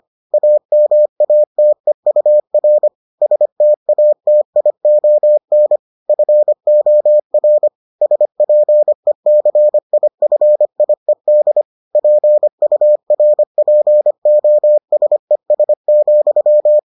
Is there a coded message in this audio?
Yes: Morse code from 0.3 to 16.8 s, heard as a keyed tone, about 25 wpm, 600 Hz. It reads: AMATEUR STATION FOR SPECIFIED PURPOSES,